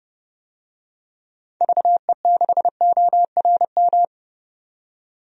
Morse code, heard as a keyed tone, about 30 wpm, 700 Hz. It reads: VE6ORM
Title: DXCC-100 Call Signs - 6 Characters - 30wpm